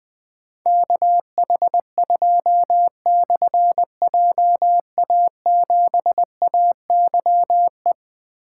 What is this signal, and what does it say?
Morse code, heard as a keyed tone, about 20 wpm, 700 Hz. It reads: KH2/JA7AYE
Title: Difficult Call Signs 20wpm